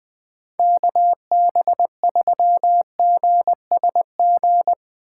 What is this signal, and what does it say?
Morse code, heard as a keyed tone, about 21 wpm, 700 Hz. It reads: KB3GSG
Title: Speed Racing - USA 2x3 Call Signs 20wpm